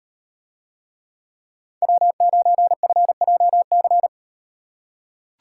Morse code, heard as a keyed tone, about 38 wpm, 700 Hz. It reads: W9FJC